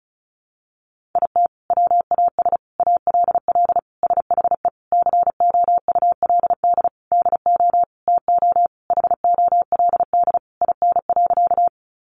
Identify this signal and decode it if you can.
Morse code, heard as a keyed tone, about 34 wpm, 700 Hz. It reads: IT WAS ALL SHE COULD DO TO HOLD IN.
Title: Sentences from Top 400 Words 35wpm